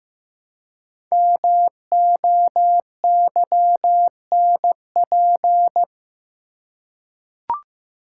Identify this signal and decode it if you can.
Morse code, heard as a keyed tone, about 15 wpm, 700 Hz. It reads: MOYNP
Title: Code Groups - 5 Letter:Numbers - 15wpm